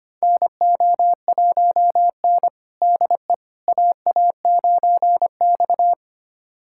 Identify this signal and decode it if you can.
Morse code, heard as a keyed tone, about 25 wpm, 700 Hz. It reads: NO1N DE AA9X
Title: USA Callsign DE Callsign - 4 Characters - 25wpm